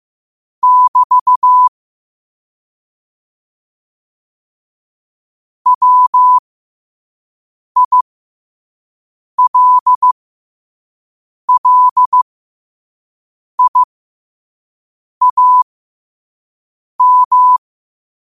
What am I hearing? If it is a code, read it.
Morse code, heard as a keyed tone, about 15 wpm, spaced as for 5 wpm, 1000 Hz. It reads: <BT> WILLIAM
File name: qotdfiles / 5wpm